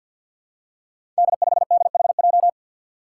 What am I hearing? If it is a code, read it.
Morse code, heard as a keyed tone, about 51 wpm, 700 Hz. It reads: D5DHJ
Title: Five Challenging Letter-Number Combinations 50wpm